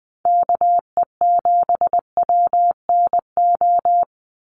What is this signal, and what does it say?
Morse code, heard as a keyed tone, about 20 wpm, 700 Hz. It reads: KE7WNO
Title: USA Call Signs 20wpm